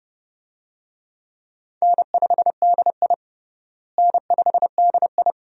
Morse code, heard as a keyed tone, about 30 wpm, 700 Hz. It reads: N5DI N5DI
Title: USA Call Signs - 4 Characters - Repeated 2x - 30wpm